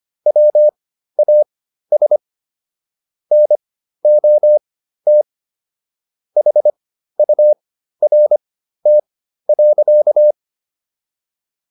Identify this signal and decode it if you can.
Morse code, heard as a keyed tone, about 25 wpm, spaced as for 13 wpm, 600 Hz. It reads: WAS NOT HURT.